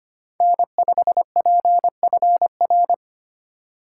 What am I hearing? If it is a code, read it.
Morse code, heard as a keyed tone, about 25 wpm, 700 Hz. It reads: N5PFR
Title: USA Call Signs - 5 Characters - 4x Character Spacing - 25wpm